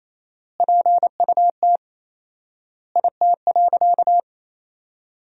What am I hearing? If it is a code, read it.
Morse code, heard as a keyed tone, about 28 wpm, 700 Hz. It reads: PUT IT.
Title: 4x Word-Spacing - Sentences from Top 200 Words - 28wpm